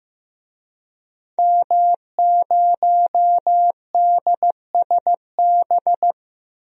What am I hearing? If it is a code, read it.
Morse code, heard as a keyed tone, about 15 wpm, 700 Hz. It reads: M0DSB
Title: DXCC-100 Call Signs - 5 Characters - 15wpm